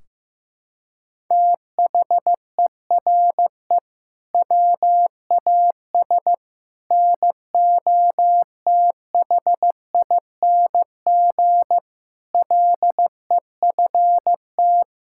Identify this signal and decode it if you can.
Morse code, heard as a keyed tone, about 15 wpm, 700 Hz. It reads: THERE WAS NOTHING LEFT
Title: Sentences from Top 1600 Words 15wpm